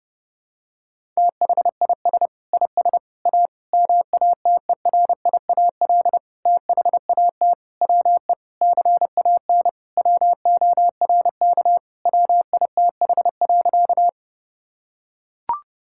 Morse code, heard as a keyed tone, about 30 wpm, 700 Hz. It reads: THIS IS A MATERIAL THAT WE CAN WORK WITH.
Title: Sentences from Top 1000 Words - Review Part 5 - 30wpm